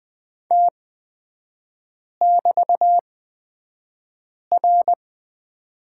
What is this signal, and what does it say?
Morse code, heard as a keyed tone, about 20 wpm, spaced as for 5 wpm, 700 Hz. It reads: T<BT>R
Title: 250 Lesson 40e - BK - Three Characters 20wpm